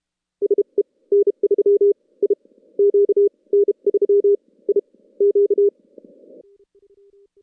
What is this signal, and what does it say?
Morse code, heard as a keyed tone, about 36 wpm, spaced as for 22 wpm, 400 Hz. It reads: SE N3IQ N3IQ
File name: contests / CW-SS-k5tr- -2300